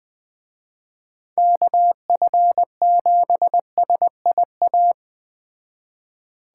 Morse code, heard as a keyed tone, about 21 wpm, 700 Hz. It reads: KF7SIA
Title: Speed Racing - USA 2x3 Call Signs 17wpm